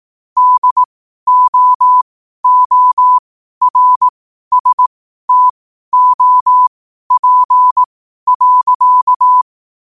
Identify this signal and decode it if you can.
Morse code, heard as a keyed tone, about 18 wpm, spaced as for 12 wpm, 1000 Hz. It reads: DOORSTOP.
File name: qotdfiles / 12wpm